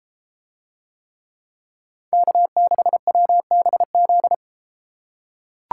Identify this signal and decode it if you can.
Morse code, heard as a keyed tone, about 33 wpm, 700 Hz. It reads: K6WBZ